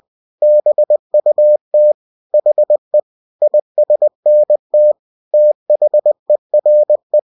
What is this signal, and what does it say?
Morse code, heard as a keyed tone, about 20 wpm, 600 Hz. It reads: BUT HE ISNT THERE